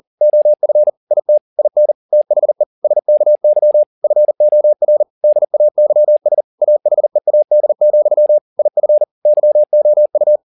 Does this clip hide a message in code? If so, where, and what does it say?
Yes: Morse code from 0.2 to 10.4 s, heard as a keyed tone, about 40 wpm, 600 Hz. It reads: OF IT IN THE SKY FOR DAYS AHEAD, IF YOU